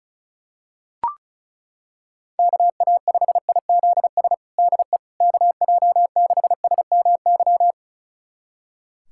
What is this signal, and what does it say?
Morse code, heard as a keyed tone, about 36 wpm, 700 Hz. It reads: KA5IZS DE KJ6SMY
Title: USA Callsign DE Callsign - 6 Characters - 35wpm